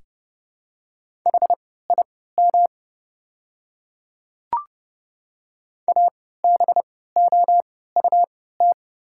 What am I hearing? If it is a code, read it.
Morse code, heard as a keyed tone, about 30 wpm, spaced as for 17 wpm, 700 Hz. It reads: HIM ABOUT
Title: 3x Character Spacing - Top 100 Words - 30wpm